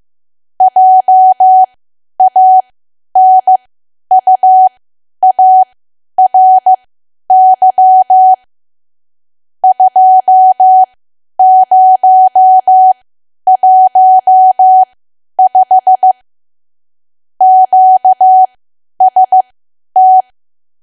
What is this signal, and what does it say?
Morse code, heard as a keyed tone, about 15 wpm, spaced as for 10 wpm, 750 Hz. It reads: JANUARY 2015 QST
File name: Archive / 10 WPM / 10WPM